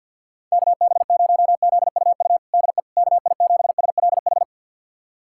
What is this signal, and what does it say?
Morse code, heard as a keyed tone, about 50 wpm, 700 Hz. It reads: KB0ZUU DE KI7SLH
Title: USA Callsign DE Callsign - 6 Characters - 50wpm